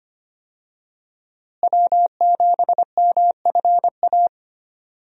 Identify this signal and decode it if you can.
Morse code, heard as a keyed tone, about 25 wpm, 700 Hz. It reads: W7MFA
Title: USA Call Signs - 5 Characters - Repeated 2x - 25wpm